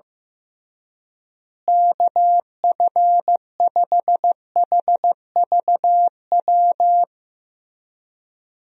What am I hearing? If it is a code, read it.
Morse code, heard as a keyed tone, about 15 wpm, 700 Hz. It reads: KF5HVW
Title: USA Call Signs - 6 Characters - Repeated 3x - 15wpm